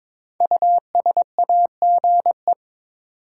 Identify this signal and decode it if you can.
Morse code, heard as a keyed tone, about 22 wpm, 700 Hz. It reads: USAGE
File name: 5 Letter Words - First 3 Letters Given - 22wpm